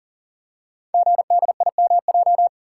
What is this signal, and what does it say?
Morse code, heard as a keyed tone, about 39 wpm, 700 Hz. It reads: GDIMJ
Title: Code Groups - 5 Letters - 40wpm